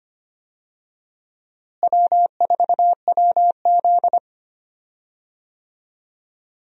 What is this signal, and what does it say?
Morse code, heard as a keyed tone, about 25 wpm, 700 Hz. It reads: W4WZ
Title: USA Call Signs - 4 Characters - Repeated 3x - 25wpm